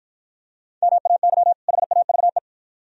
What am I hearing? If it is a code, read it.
Morse code, heard as a keyed tone, about 53 wpm, 690 Hz. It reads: MAY HAVE